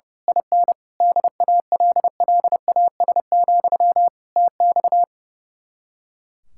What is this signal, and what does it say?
Morse code, heard as a keyed tone, about 29 wpm, 700 Hz. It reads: IN DALLAS, TX
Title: Speed-Racing Common QSO Phrases Encore - 30wpm